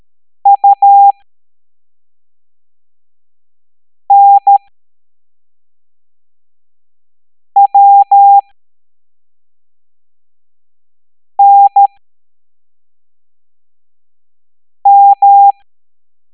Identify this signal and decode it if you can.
Morse code, heard as a keyed tone, about 13 wpm, spaced as for 3 wpm, 800 Hz. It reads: UNWNM